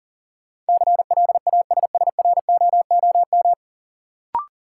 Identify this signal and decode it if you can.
Morse code, heard as a keyed tone, about 37 wpm, 700 Hz. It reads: CLASSROOM
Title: Sentences from Top 1500 Words - New Word Drill Alt - 40wpm